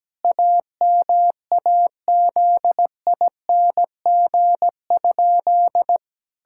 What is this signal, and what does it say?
Morse code, heard as a keyed tone, about 17 wpm, 700 Hz. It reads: AMAZING?
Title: Pangram Challenge v2 - 17wpm